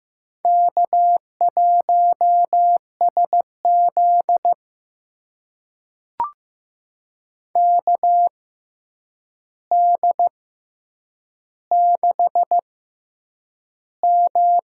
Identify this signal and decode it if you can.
Morse code, heard as a keyed tone, about 15 wpm, spaced as for 8 wpm, 700 Hz. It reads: K1SZ KD6M
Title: USA Call Signs - 4 Characters - 6x Character Spacing - 15wpm